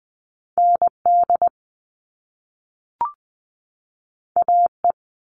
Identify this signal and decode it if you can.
Morse code, heard as a keyed tone, about 20 wpm, 700 Hz. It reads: ND AE
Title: Top Two Letter Combinations in Words 20wpm